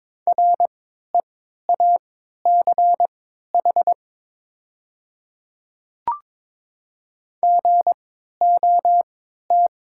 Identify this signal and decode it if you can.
Morse code, heard as a keyed tone, about 22 wpm, spaced as for 12 wpm, 700 Hz. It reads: REACH GOT
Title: 3x Character Spacing - Top 500 Words - 22wpm